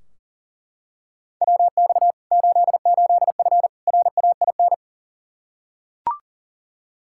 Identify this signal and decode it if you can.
Morse code, heard as a keyed tone, about 38 wpm, 700 Hz. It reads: WX 88F RAIN
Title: Practice QSO - Exchange 2 - Info - 40wpm